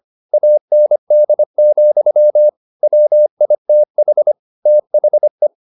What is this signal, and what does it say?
Morse code, heard as a keyed tone, about 25 wpm, 600 Hz. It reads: AND, WITH THE